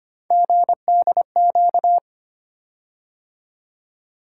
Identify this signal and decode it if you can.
Morse code, heard as a keyed tone, about 25 wpm, 700 Hz. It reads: GDQ
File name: Three Letter-Number Combinations - 25wpm